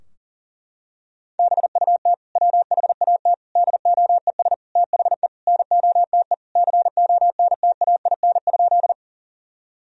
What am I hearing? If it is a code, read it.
Morse code, heard as a keyed tone, about 40 wpm, 700 Hz. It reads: BUT WHAT DOES THE NOTE CONTAIN?
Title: Sentences from Top 1000 Words - Review Part 2 - 40wpm